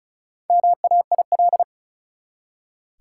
Morse code, heard as a keyed tone, about 36 wpm, 700 Hz. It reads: MAIL
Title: Four Letter Words - 35wpm